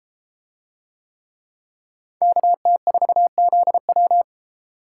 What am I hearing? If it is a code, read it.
Morse code, heard as a keyed tone, about 31 wpm, 700 Hz. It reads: KT4ZW